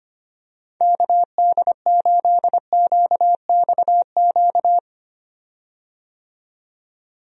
Morse code, heard as a keyed tone, about 25 wpm, 700 Hz. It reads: KD8QXQ
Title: USA Call Signs - 6 Characters - 25wpm